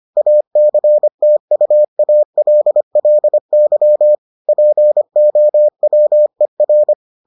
Morse code, heard as a keyed tone, about 25 wpm, 600 Hz. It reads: ACTUALLY POWER